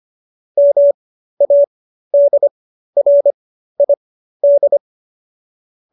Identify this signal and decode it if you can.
Morse code, heard as a keyed tone, about 25 wpm, spaced as for 13 wpm, 570 Hz. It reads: MADRID
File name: qso215b 13